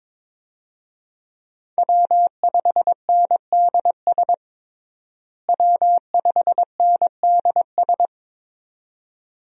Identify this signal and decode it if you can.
Morse code, heard as a keyed tone, about 22 wpm, 700 Hz. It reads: W5NDS W5NDS
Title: USA Call Signs - 5 Characters - Repeated 2x - 22wpm